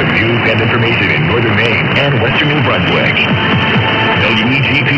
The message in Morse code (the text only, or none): none